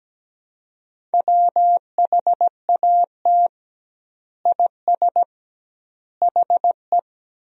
Morse code, heard as a keyed tone, about 17 wpm, 700 Hz. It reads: WHAT IS HE
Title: 2x Word-Spacing - Sentences from Top 100 Words - 17wpm